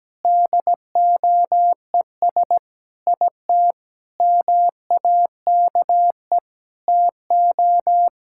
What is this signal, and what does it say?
Morse code, heard as a keyed tone, about 17 wpm, 700 Hz. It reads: DOES IT MAKE TO
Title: Sentences from Top 1000 Words - Review Part 5 - 17wpm